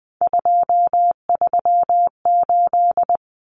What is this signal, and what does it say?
Morse code, heard as a keyed tone, about 20 wpm, 700 Hz. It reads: 238
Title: Common Cut Numbers - 3-Digit 20wpm